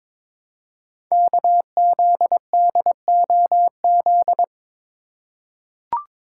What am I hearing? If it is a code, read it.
Morse code, heard as a keyed tone, about 23 wpm, 690 Hz. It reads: KZDOZ
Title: Code Groups - 5 Letter:Numbers - 22wpm